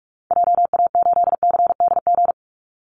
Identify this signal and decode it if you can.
Morse code, heard as a keyed tone, about 44 wpm, 700 Hz. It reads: JA8CDG
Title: Call Signs Repeated 2x - 45wpm